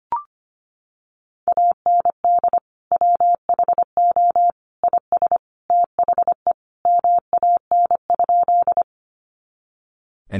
Morse code, heard as a keyed tone, about 25 wpm, 700 Hz. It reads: AND WHO IS THE MAN?
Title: Sentences from Top 200 Words 25wpm